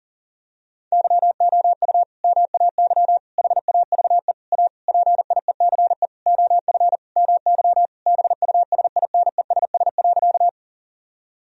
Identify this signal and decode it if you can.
Morse code, heard as a keyed tone, about 40 wpm, 700 Hz. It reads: YOU MAY HAVE A PIECE OF MY BUSINESS.
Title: Sentences from Top 1000 Words - Review Part 2 - 40wpm